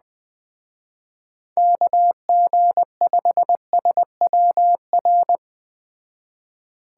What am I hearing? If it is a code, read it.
Morse code, heard as a keyed tone, about 20 wpm, 700 Hz. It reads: KG5SWR